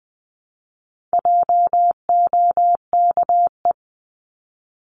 Morse code, heard as a keyed tone, about 20 wpm, 700 Hz. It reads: JOKE
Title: 260 Lesson 42e - Words 20-20wpm